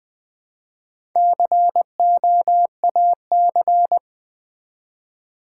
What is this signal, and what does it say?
Morse code, heard as a keyed tone, about 20 wpm, 700 Hz. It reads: COAC
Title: Build A Word - 1 Letter At A Time - 3 to 5-Letter Words - 20wpm